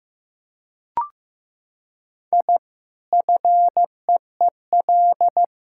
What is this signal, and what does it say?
Morse code, heard as a keyed tone, about 15 wpm, 700 Hz. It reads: I FEEL
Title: Sentences from Top 1000 Words - Review Part 6 - 15wpm